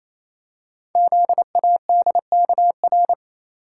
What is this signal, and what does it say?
Morse code, heard as a keyed tone, about 29 wpm, 700 Hz. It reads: ZADKR
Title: Code Groups - 5 Letters - 28wpm